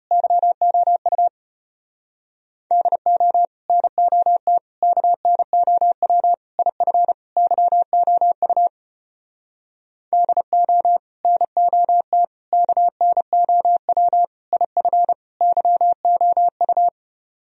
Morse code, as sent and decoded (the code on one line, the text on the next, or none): -.-- --- ..- / -.. --- / -. --- - / -.- -. --- .-- / .. ..-. / -.-- --- ..- / -.. --- / -. --- - / -.- -. --- .-- / .. ..-. / -.-- --- ..-
YOU DO NOT KNOW IF YOU DO NOT KNOW IF YOU